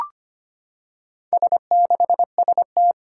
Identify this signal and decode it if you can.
Morse code, heard as a keyed tone, about 25 wpm, 700 Hz. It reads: S6ST
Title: Four Challenging Letter-Number Combinations 25wpm